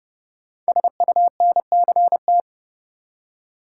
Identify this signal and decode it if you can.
Morse code, heard as a keyed tone, about 30 wpm, 700 Hz. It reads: SUNCT
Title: Code Groups - 5 Letter:Numbers - 30wpm